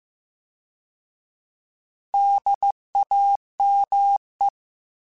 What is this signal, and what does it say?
Morse code, heard as a keyed tone, about 15 wpm, 800 Hz. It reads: DAME
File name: Alphabetical / Alphabetical mp3s / Ds @13